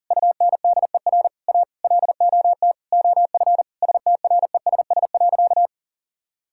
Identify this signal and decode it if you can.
Morse code, heard as a keyed tone, about 37 wpm, 700 Hz. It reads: UNDER A LOT OF STRESS.
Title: Sentences from Top 1000 Words - Review Part 6 - 40wpm